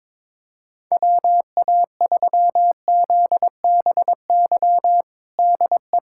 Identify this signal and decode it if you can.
Morse code, heard as a keyed tone, about 22 wpm, 700 Hz. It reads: WA3ZBY DE
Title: USA Callsign DE Callsign 22wpm